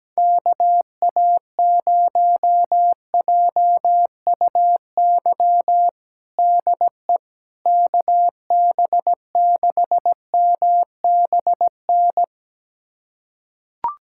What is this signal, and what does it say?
Morse code, heard as a keyed tone, about 17 wpm, 700 Hz. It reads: KA0JUY DE KB6MBN
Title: USA Callsign DE Callsign 17wpm